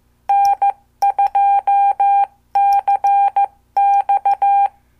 Morse code, heard as a keyed tone, about 14 wpm, 800 Hz. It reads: N2CX
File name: morse / qso